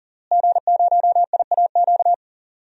Morse code, heard as a keyed tone, about 37 wpm, 700 Hz. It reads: G0IAQ
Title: DXCC-100 Call Signs - 5 Characters - Repeated 3x - 40wpm